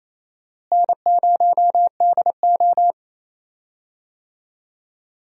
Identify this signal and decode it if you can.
Morse code, heard as a keyed tone, about 28 wpm, 700 Hz. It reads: N0DO
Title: USA Call Signs - 4 Characters - Repeated 2x - 28wpm